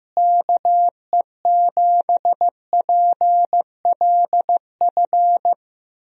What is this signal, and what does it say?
Morse code, heard as a keyed tone, about 15 wpm, 700 Hz. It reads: KE7PLF
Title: USA Call Signs - 6 Characters - Repeated 3x - 15wpm